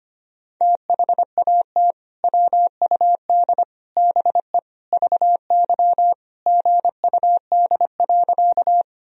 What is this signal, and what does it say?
Morse code, heard as a keyed tone, about 25 wpm, 700 Hz. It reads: THAT WUD BE VY GUD.
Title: Sentences from Top 200 Words and Abbreviations 25wpm